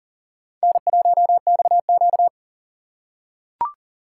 Morse code, heard as a keyed tone, about 40 wpm, 700 Hz. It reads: N1XQ
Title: USA Call Signs - 4 Characters - 2x Character Spacing - 40wpm